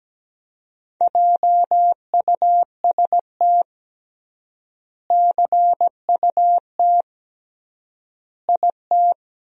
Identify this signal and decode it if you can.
Morse code, heard as a keyed tone, about 17 wpm, 700 Hz. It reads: JUST CUT IT
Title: 3x Word-Spacing - Sentences from Top 300 Words - 17wpm